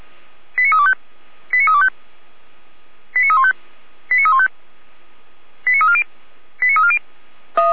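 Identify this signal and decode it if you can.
Five-tone selective call (ZVEI), heard as quick runs of five tones, tones of about 70 ms. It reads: 89325 89325 89315 89315 89340 89340
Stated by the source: ZVEI five-tone selective call received on an SDR, archived by fire brigade Orlamünde (feuerwehr-orlamuende).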